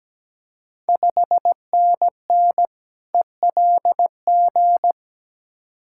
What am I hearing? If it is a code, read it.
Morse code, heard as a keyed tone, about 17 wpm, 690 Hz. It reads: 5NN ELG
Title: Ontario QSO Party Exchange 17wpm